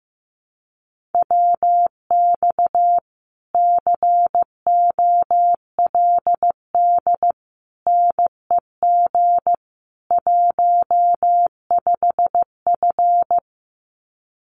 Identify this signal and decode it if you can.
Morse code, heard as a keyed tone, about 15 wpm, 700 Hz. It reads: WX COLD NEG 15F
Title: Common QSO Phrases Encore 15wpm